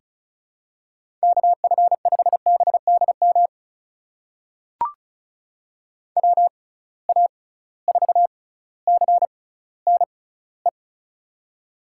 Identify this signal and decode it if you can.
Morse code, heard as a keyed tone, about 36 wpm, spaced as for 18 wpm, 700 Hz. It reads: KF5BDM WA4CNE
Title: USA Call Signs - 6 Characters - 6x Character Spacing - 35wpm